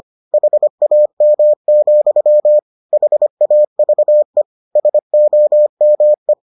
Morse code, heard as a keyed tone, about 25 wpm, 600 Hz. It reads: HAM, HAVE SOME